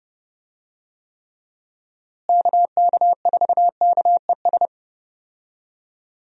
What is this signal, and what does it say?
Morse code, heard as a keyed tone, about 30 wpm, 700 Hz. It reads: KK4KES
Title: USA Call Signs - 6 Characters - Repeated 2x - 30wpm